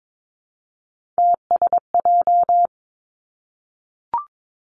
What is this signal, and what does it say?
Morse code, heard as a keyed tone, about 22 wpm, 700 Hz. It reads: TSJ
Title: Three Challenging Letter-Number Combinations 22wpm